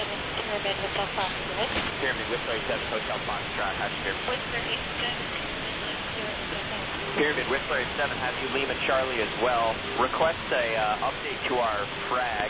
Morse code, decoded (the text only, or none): none